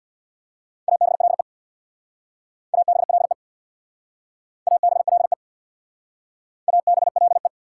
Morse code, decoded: ABLE ABLE ABLE ABLE

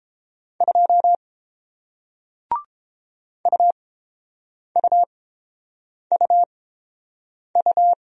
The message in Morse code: 2 UUUU

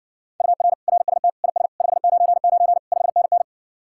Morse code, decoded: UR RST IS 599 5NN

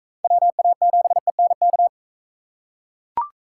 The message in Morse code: WA7ENK

